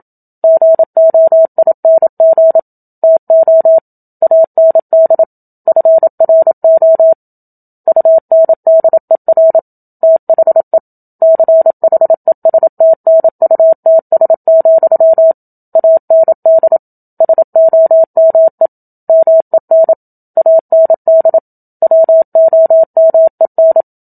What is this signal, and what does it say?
Morse code, heard as a keyed tone, about 27 wpm, 650 Hz. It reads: GOING TO AND FRO UNDER THE CHESTNUTS, AND SOME MEN AND WOMEN
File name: Book Two / 01-Chapter 1 - 27 WPM